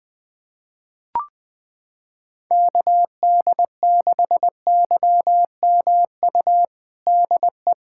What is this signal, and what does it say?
Morse code, heard as a keyed tone, about 20 wpm, 700 Hz. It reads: KD6YMU DE
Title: USA Callsign DE Callsign - 6 Characters - 20wpm